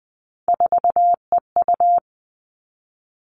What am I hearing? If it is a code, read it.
Morse code, heard as a keyed tone, about 20 wpm, 700 Hz. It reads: 4EU
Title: All Cut Numbers - 3-Digit 20wpm